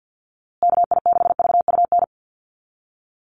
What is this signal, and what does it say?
Morse code, heard as a keyed tone, about 53 wpm, 700 Hz. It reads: KI6VUN
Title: USA Call Signs Repeated 2x 50wpm